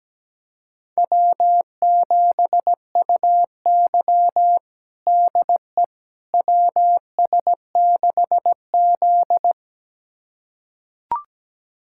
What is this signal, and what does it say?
Morse code, heard as a keyed tone, about 17 wpm, 700 Hz. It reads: W7UY DE WS6Z
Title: USA Callsign DE Callsign - 4 Characters - 17wpm